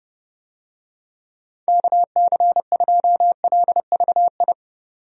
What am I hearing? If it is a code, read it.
Morse code, heard as a keyed tone, about 30 wpm, 700 Hz. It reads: KC2LVI
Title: USA Call Signs - 6 Characters - 30wpm